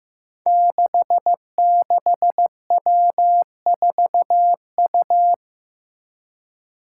Morse code, heard as a keyed tone, about 15 wpm, 700 Hz. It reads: 66W4U
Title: Five Challenging Letter-Number Combinations 15wpm